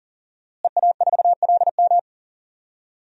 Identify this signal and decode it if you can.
Morse code, heard as a keyed tone, about 40 wpm, 700 Hz. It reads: EA4LM
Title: DXCC-100 Call Signs - 5 Characters - Repeated 3x - 40wpm